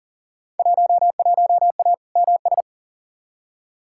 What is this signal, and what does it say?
Morse code, heard as a keyed tone, about 40 wpm, 700 Hz. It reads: 11A MS
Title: Field Day Exchanges Encore - 40wpm